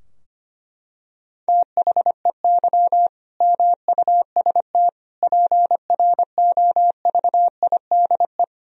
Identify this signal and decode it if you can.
Morse code, heard as a keyed tone, about 25 wpm, 700 Hz. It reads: THEY MUST PROVIDE
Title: Sentences from Top 1000 Words - Review Part 3 - 25wpm